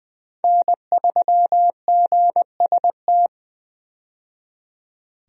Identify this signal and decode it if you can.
Morse code, heard as a keyed tone, about 20 wpm, 700 Hz. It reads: N3GST
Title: USA Call Signs - 5 Characters - 20wpm